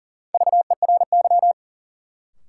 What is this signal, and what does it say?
Morse code, heard as a keyed tone, about 40 wpm, 700 Hz. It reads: VERY
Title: Sentences from Top 200 Words - New Word Drill - 40wpm